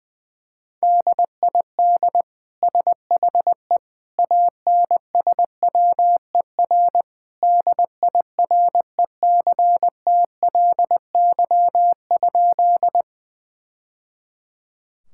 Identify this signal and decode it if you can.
Morse code, heard as a keyed tone, about 21 wpm, 700 Hz. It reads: DID SHE ANSWER DIRECTLY?
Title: Sentences from Top 1000 Words - Review Part 5 - 20wpm